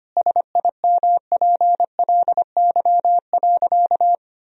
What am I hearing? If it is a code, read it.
Morse code, heard as a keyed tone, about 25 wpm, 700 Hz. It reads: SIMPLY.
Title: Sentences from Top 1000 Words - Review Part 4 - 25wpm